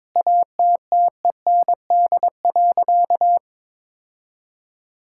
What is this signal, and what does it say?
Morse code, heard as a keyed tone, about 22 wpm, 700 Hz. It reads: ATTEND.
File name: Sentences with 7-Word Phrases - 22wpm